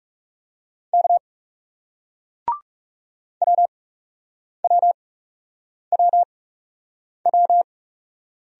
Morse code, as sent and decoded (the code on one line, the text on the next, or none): -.- / .-- .-- .-- .--
K WWWW